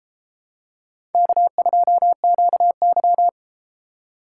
Morse code, heard as a keyed tone, about 31 wpm, 700 Hz. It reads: K2QY